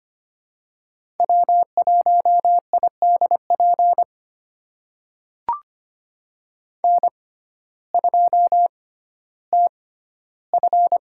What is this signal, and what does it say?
Morse code, heard as a keyed tone, about 25 wpm, spaced as for 14 wpm, 700 Hz. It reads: W1IDP N2TF
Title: USA Call Signs - 5 Characters - 6x Character Spacing - 25wpm